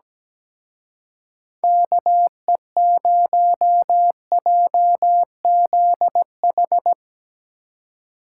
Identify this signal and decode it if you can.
Morse code, heard as a keyed tone, about 17 wpm, 700 Hz. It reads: KE0JZH
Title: Speed Racing - USA 2x3 Call Signs 17wpm